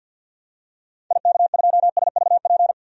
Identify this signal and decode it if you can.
Morse code, heard as a keyed tone, about 48 wpm, 700 Hz. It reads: IK2SVP
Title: DXCC-100 Call Signs - 6 Characters - 50wpm